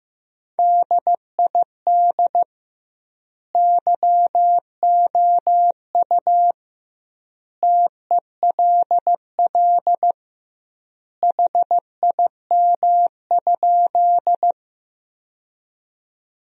Morse code, heard as a keyed tone, about 15 wpm, 700 Hz. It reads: DID YOU TELL HIM?
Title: 2x Word-Spacing - Sentences from Top 200 Words - 15wpm